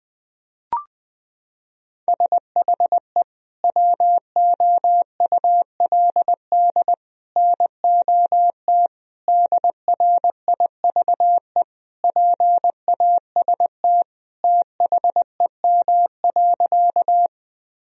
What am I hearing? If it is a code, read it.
Morse code, heard as a keyed tone, about 20 wpm, 700 Hz. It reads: SHE WOULD NOT DRIVE PAST THEM.
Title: Sentences from Top 1000 Words - Review Part 3 - 20wpm